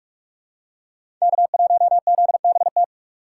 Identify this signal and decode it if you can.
Morse code, heard as a keyed tone, about 46 wpm, 690 Hz. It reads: K1ZBT
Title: USA Call Signs - 5 Characters - 6x Character Spacing - 45wpm